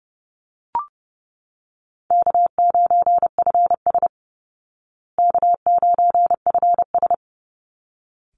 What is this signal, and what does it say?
Morse code, heard as a keyed tone, about 28 wpm, 700 Hz. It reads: K9FS K9FS
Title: Easy Call Signs Repeated 2x - 30wpm